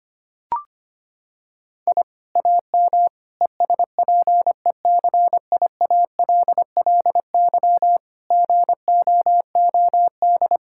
Morse code, heard as a keyed tone, about 25 wpm, 700 Hz. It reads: I AM ESPECIALLY GOOD
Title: Sentences from Top 1000 Words - Review Part 6 - 25wpm